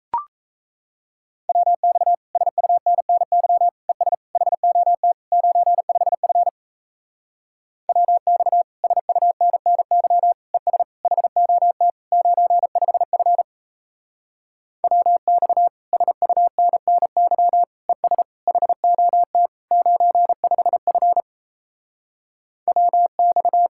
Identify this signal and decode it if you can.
Morse code, heard as a keyed tone, about 36 wpm, 700 Hz. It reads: WX SUNNY ES HOT 95F WX SUNNY ES HOT 95F WX SUNNY ES HOT 95F WX